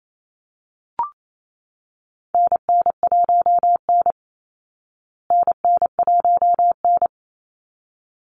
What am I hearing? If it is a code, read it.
Morse code, heard as a keyed tone, about 28 wpm, 700 Hz. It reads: NN1N NN1N
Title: Call Signs Repeated 3x - 28wpm